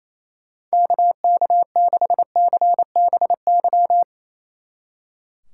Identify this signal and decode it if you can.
Morse code, heard as a keyed tone, about 34 wpm, 700 Hz. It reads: KK6CBY
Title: Speed Racing - USA 2x3 Call Signs 28wpm